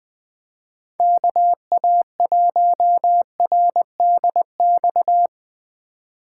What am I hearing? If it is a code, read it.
Morse code, heard as a keyed tone, about 20 wpm, 700 Hz. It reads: KA1RDX